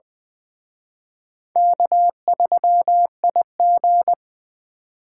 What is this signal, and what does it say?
Morse code, heard as a keyed tone, about 24 wpm, 700 Hz. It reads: K3IG
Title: Speed Racing - USA 1x2 Call Signs 20wpm